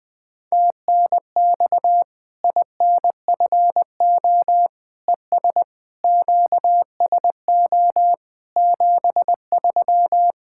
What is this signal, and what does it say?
Morse code, heard as a keyed tone, about 20 wpm, 700 Hz. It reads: TNX INFO ES QSO 73
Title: Practice QSO - Exchange 4 - Ending - 20wpm